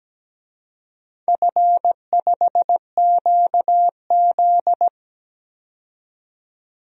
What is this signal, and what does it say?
Morse code, heard as a keyed tone, about 17 wpm, 700 Hz. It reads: F5QZ
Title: DXCC-100 Call Signs - 4 Characters - 17wpm